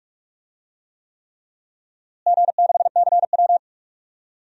Morse code, heard as a keyed tone, about 41 wpm, 700 Hz. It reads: G6CW
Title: DXCC-100 Call Signs - 4 Characters - Repeated 3x - 45wpm